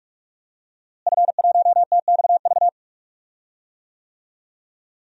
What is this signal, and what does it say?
Morse code, heard as a keyed tone, about 46 wpm, 700 Hz. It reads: F1TXV
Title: DXCC-100 Call Signs - 5 Characters - Repeated 2x - 45wpm